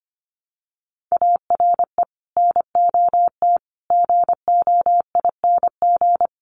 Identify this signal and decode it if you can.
Morse code, heard as a keyed tone, about 25 wpm, 700 Hz. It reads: ARE NOT GOING
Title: Sets of 5 Words Encore 25wpm